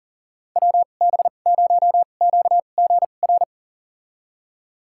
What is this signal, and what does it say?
Morse code, heard as a keyed tone, about 37 wpm, spaced as for 28 wpm, 700 Hz. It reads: WB0QGR
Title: USA Call Signs - 6 Characters - 2x Character Spacing - 40wpm